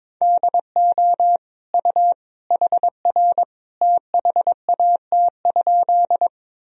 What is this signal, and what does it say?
Morse code, heard as a keyed tone, about 22 wpm, 700 Hz. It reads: DO U HR THAT?
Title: Sentences from Top 300 Words and Abbreviations 22wpm